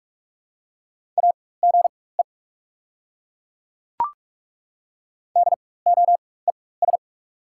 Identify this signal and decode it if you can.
Morse code, heard as a keyed tone, about 43 wpm, spaced as for 21 wpm, 690 Hz. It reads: AGE DOES
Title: 4x Character Spacing - Top 500 Words - 45wpm